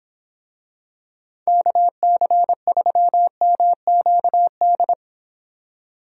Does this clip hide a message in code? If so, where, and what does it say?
Yes: Morse code from 1.5 to 4.9 s, heard as a keyed tone, about 25 wpm, 700 Hz. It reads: KC3MQD